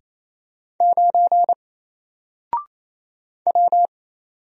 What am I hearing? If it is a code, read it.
Morse code, heard as a keyed tone, about 27 wpm, 700 Hz. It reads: 9 W
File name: Single-Letter-Number-28wpm